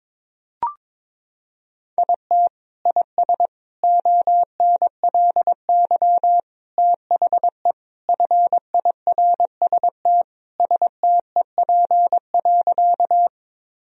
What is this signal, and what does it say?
Morse code, heard as a keyed tone, about 22 wpm, 700 Hz. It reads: IT IS ONLY THE FIRST STEP.
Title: Sentences from Top 1000 Words - Review Part 2 - 22wpm